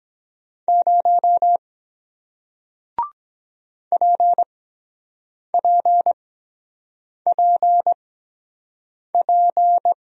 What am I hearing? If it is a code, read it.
Morse code, heard as a keyed tone, about 23 wpm, 700 Hz. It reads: 0 P P P P